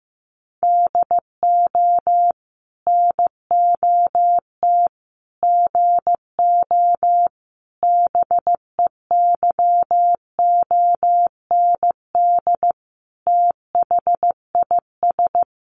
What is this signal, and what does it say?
Morse code, heard as a keyed tone, about 15 wpm, 700 Hz. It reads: DO NOT GO BEYOND THIS
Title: Sentences from Top 1500 Words 15wpm